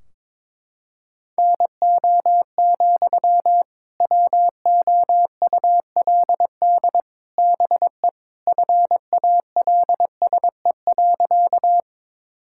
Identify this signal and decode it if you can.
Morse code, heard as a keyed tone, about 22 wpm, 700 Hz. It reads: NO, WOULD BE FALSE.
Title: Sentences from Top 1600 Words 22wpm